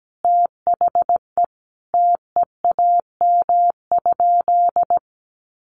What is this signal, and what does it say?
Morse code, heard as a keyed tone, about 17 wpm, 700 Hz. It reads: THE TEAM?
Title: Sentences from Top 1500 Words 17wpm